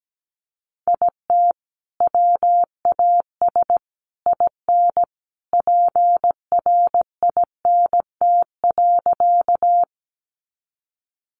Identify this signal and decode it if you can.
Morse code, heard as a keyed tone, about 17 wpm, 700 Hz. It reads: IT WAS IN PRINT.
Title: Sentences from Top 1400 Words 17wpm